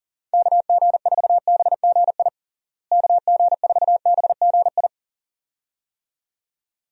Morse code, heard as a keyed tone, about 40 wpm, 700 Hz. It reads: KG4BGI KG4BGI
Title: USA Call Signs - 6 Characters - Repeated 2x - 40wpm